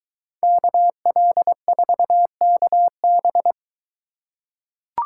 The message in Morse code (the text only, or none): KL4KB